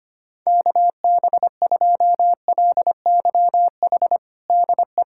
Morse code, heard as a keyed tone, about 25 wpm, 700 Hz. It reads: KB2LYH DE
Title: USA Callsign DE Callsign 25wpm